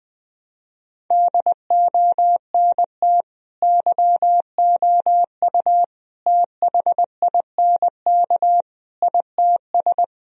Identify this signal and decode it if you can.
Morse code, heard as a keyed tone, about 20 wpm, 700 Hz. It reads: DONT YOU THINK ITS
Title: Sentences from Top 1000 Words - Review Part 4 - 20wpm